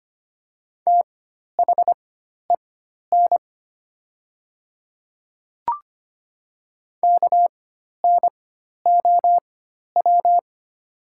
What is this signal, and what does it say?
Morse code, heard as a keyed tone, about 25 wpm, spaced as for 12 wpm, 700 Hz. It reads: THEN KNOW
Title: 4x Character Spacing - Top 200 Words - 25wpm